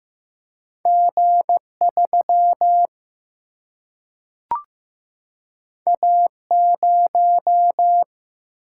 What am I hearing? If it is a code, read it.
Morse code, heard as a keyed tone, about 15 wpm, 700 Hz. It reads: G3 A0
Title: Single-Letter Single-Number 15wpm